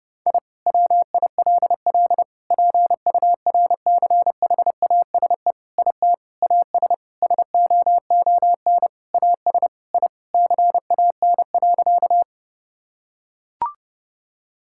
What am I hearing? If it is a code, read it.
Morse code, heard as a keyed tone, about 32 wpm, 700 Hz. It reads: I WILL PURCHASE IT AS SOON AS I CAN.
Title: Sentences from Top 1000 Words - Review Part 5 - 30wpm